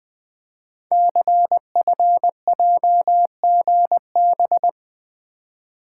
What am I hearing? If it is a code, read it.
Morse code, heard as a keyed tone, about 20 wpm, 700 Hz. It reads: CFJGB
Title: Code Groups - 5 Letters - 20wpm